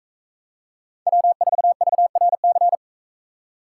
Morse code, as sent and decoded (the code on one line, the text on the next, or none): .-- ....- ...- .-. -.-.
W4VRC